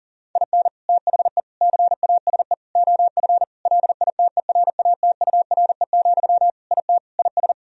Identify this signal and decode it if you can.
Morse code, heard as a keyed tone, about 40 wpm, 700 Hz. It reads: IN THE CASE OF LITERATURE, IT IS
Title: Sentences from Top 1000 Words - Review Part 3 - 40wpm